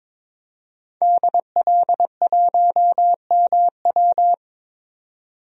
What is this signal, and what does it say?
Morse code, heard as a keyed tone, about 21 wpm, 700 Hz. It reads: DL1MW
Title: DXCC-100 Call Signs - 5 Characters - 22wpm